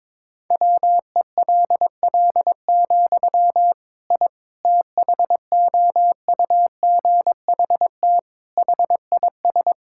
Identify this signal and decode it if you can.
Morse code, heard as a keyed tone, about 22 wpm, 700 Hz. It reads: WELL, I THOUGHT HIS
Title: Sentences from Top 1000 Words - Review Part 5 - 22wpm